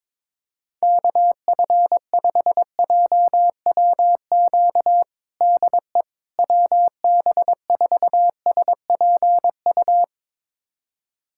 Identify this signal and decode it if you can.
Morse code, heard as a keyed tone, about 22 wpm, 700 Hz. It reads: KF5JWQ DE WB4SPU
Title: USA Callsign DE Callsign - 6 Characters - 22wpm